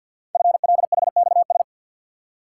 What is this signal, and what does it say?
Morse code, heard as a keyed tone, about 46 wpm, 700 Hz. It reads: FLHXS
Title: Code Groups - 5 Letters - 50wpm